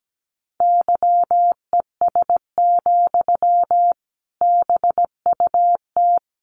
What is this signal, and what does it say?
Morse code, heard as a keyed tone, about 17 wpm, 700 Hz. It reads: YES, BUT
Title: Sentences from Top 800 Words 17wpm